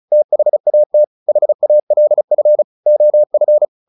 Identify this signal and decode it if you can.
Morse code, heard as a keyed tone, about 36 wpm, 600 Hz. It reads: THAT HALF OF
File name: the old man and the sea 2 35wpm